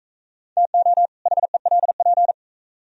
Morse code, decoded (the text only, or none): TO HELP